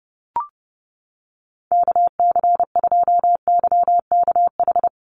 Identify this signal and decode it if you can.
Morse code, heard as a keyed tone, about 30 wpm, 700 Hz. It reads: KC2YKH
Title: USA Call Signs Repeated 2x 30wpm